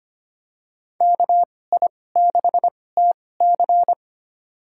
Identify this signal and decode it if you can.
Morse code, heard as a keyed tone, about 25 wpm, spaced as for 18 wpm, 700 Hz. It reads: KI6TC
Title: USA Call Signs - 2x Character Spacing - 25wpm